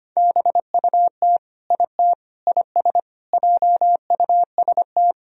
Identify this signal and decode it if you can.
Morse code, heard as a keyed tone, about 25 wpm, 700 Hz. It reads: BUT IT IS JUST
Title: Sentences from Top 200 Words and Abbreviations 25wpm